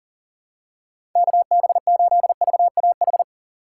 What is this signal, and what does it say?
Morse code, heard as a keyed tone, about 39 wpm, 700 Hz. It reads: KB8VAH
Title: USA Call Signs - 2x Character Spacing - 40wpm